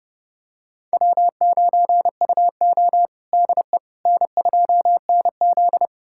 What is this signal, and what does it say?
Morse code, heard as a keyed tone, about 30 wpm, 700 Hz. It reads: W9UO DE N2NZ
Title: USA Callsign DE Callsign - 4 Characters - 30wpm